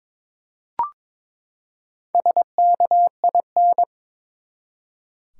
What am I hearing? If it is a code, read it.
Morse code, heard as a keyed tone, about 21 wpm, 700 Hz. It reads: SKIN
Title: Four Letter Words - 22wpm